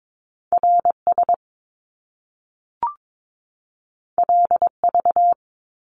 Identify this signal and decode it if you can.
Morse code, heard as a keyed tone, about 22 wpm, 700 Hz. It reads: RS LV
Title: Top Two Letter Combinations in Words 22wpm